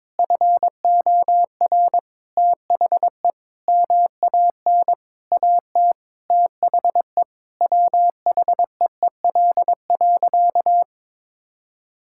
Morse code, heard as a keyed tone, about 22 wpm, 700 Hz. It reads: FOR THE MAN AT THE WHEEL.
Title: Sentences from Top 1000 Words - Review Part 5 - 22wpm